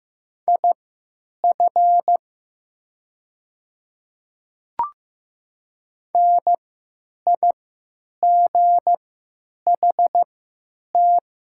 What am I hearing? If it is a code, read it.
Morse code, heard as a keyed tone, about 15 wpm, spaced as for 9 wpm, 700 Hz. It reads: IF NIGHT
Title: 3x Character Spacing - Top 300 Words - 15wpm